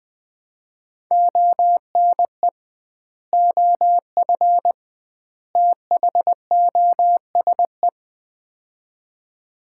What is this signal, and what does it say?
Morse code, heard as a keyed tone, about 21 wpm, 700 Hz. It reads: ONE OF THOSE
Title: 2x Word-Spacing - Sets of 3 Words - 20wpm